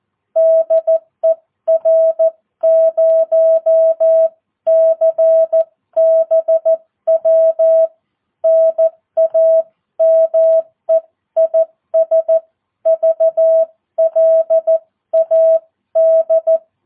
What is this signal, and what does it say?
Morse code, heard as a keyed tone, about 14 wpm, 650 Hz. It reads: DE R0CBW NAME IS VLAD
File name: jh8cbh / 2022cw